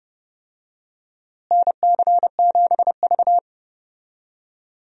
Morse code, heard as a keyed tone, about 30 wpm, 700 Hz. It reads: NC7V
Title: USA Call Signs - 4 Characters - Repeated 2x - 30wpm